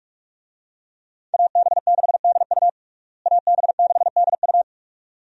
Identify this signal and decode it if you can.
Morse code, heard as a keyed tone, about 41 wpm, 700 Hz. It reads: AB6DU AB6DU
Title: USA Call Signs - 5 Characters - Repeated 2x - 45wpm